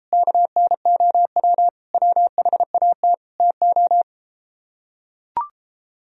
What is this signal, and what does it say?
Morse code, heard as a keyed tone, about 33 wpm, 700 Hz. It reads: KNOW WHAT TO